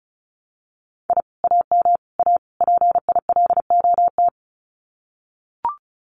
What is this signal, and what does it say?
Morse code, heard as a keyed tone, about 34 wpm, 700 Hz. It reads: I AM A PILOT
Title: Common QSO Phrases Encore 35wpm